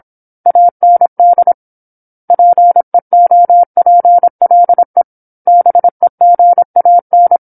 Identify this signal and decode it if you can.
Morse code, heard as a keyed tone, about 27 wpm, 700 Hz. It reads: AND PEOPLE BEGAN
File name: 08-Chapter-08 26 WPM